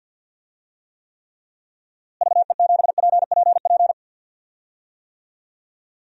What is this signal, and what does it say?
Morse code, heard as a keyed tone, about 52 wpm, 700 Hz. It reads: VE7PPP
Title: DXCC-100 Call Signs - 6 Characters - Repeated 3x - 50wpm